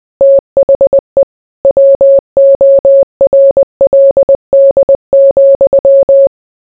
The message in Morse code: THE WORLD,

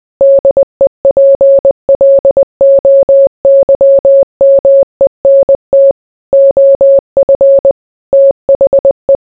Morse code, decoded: DEPLOYMENT OF THE

